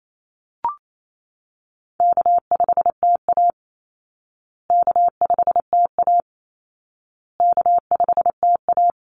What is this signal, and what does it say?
Morse code, heard as a keyed tone, about 28 wpm, 700 Hz. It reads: K5TA K5TA K5TA
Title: Call Signs Repeated 2x - 28wpm